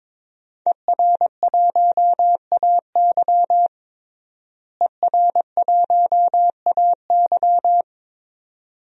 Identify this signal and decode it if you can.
Morse code, heard as a keyed tone, about 22 wpm, 700 Hz. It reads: ER1AY ER1AY
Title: DXCC-100 Call Signs - 5 Characters - Repeated 2x - 22wpm